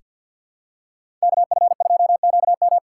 Morse code, heard as a keyed tone, about 46 wpm, 700 Hz. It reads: KF2QM
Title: USA Call Signs - 5 Characters - 3x Character Spacing - 50wpm